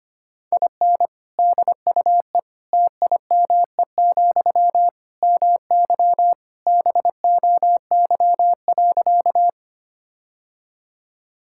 Morse code, heard as a keyed tone, about 25 wpm, 700 Hz. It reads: IN DUE TIME, MY BOY.
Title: Sentences from Top 1000 Words - Review Part 6 - 25wpm